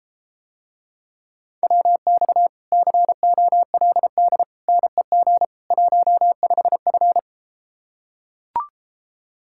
Morse code, heard as a keyed tone, about 33 wpm, 700 Hz. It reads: WX COLD NEG 15F